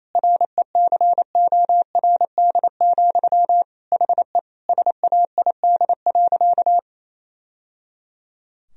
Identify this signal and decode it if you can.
Morse code, heard as a keyed tone, about 28 wpm, 700 Hz. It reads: RECORD, HE SAID.
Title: Sentences from Top 1000 Words - Review Part 3 - 28wpm